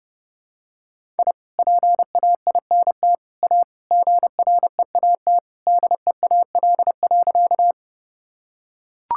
Morse code, heard as a keyed tone, about 30 wpm, 700 Hz. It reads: I PAINT A GREAT DEAL.
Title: Sentences from Top 1000 Words - Review Part 6 - 30wpm